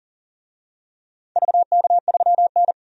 Morse code, heard as a keyed tone, about 40 wpm, 700 Hz. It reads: VK3N
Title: DXCC-100 Call Signs - 4 Characters - 40wpm